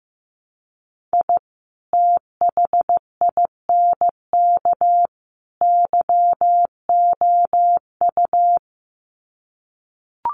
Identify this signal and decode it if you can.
Morse code, heard as a keyed tone, about 15 wpm, 700 Hz. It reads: I THINK YOU
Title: Sets of 3 Words 15wpm